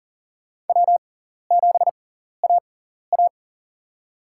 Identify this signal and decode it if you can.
Morse code, heard as a keyed tone, about 40 wpm, spaced as for 14 wpm, 700 Hz. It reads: W7AA
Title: USA Call Signs - 6x Character Spacing - 40wpm